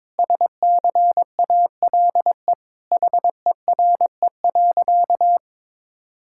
Morse code, heard as a keyed tone, about 22 wpm, 700 Hz. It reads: SCALE HERE.
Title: Sentences from Top 1000 Words - Review Part 5 - 22wpm